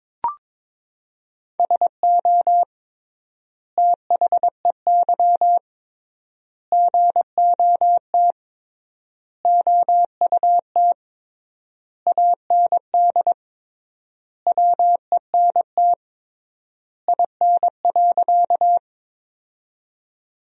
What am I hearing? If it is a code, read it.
Morse code, heard as a keyed tone, about 22 wpm, 700 Hz. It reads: SO THEY GOT OUT AND WENT IN.
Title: 3x Word-Spacing - Sentences from Top 300 Words - 22wpm